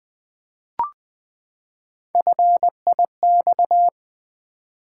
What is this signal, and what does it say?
Morse code, heard as a keyed tone, about 20 wpm, 700 Hz. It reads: FIX
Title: Three Letter Words - 20wpm